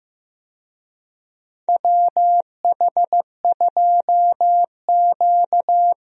Morse code, heard as a keyed tone, about 15 wpm, 700 Hz. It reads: WH2Q
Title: USA Call Signs - 4 Characters - Repeated 3x - 15wpm